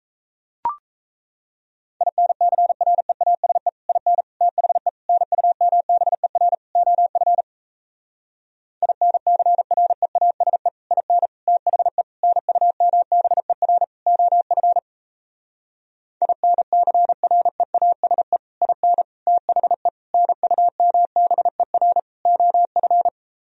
Morse code, heard as a keyed tone, about 38 wpm, 700 Hz. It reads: INCREASE IN THE NUMBER OF INCREASE IN THE NUMBER OF INCREASE IN THE NUMBER OF